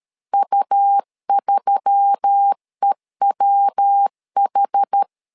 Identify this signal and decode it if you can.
Morse code, heard as a keyed tone, about 13 wpm, 800 Hz. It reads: U3EWH
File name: LEZIONI / 4 Q1S3 12wpm 2023